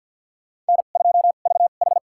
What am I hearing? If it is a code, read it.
Morse code, heard as a keyed tone, about 50 wpm, spaced as for 36 wpm, 700 Hz. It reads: N2VH
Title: USA Call Signs - 2x Character Spacing - 50wpm